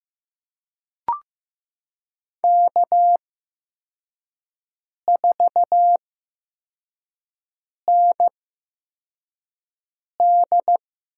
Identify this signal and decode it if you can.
Morse code, heard as a keyed tone, about 15 wpm, spaced as for 4 wpm, 700 Hz. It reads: K4ND
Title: USA Call Signs - 8x Character Spacing - 15wpm